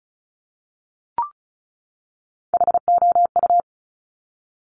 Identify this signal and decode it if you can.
Morse code, heard as a keyed tone, about 33 wpm, 700 Hz. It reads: HOU
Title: Top Three Letter Combinations in Words 35wpm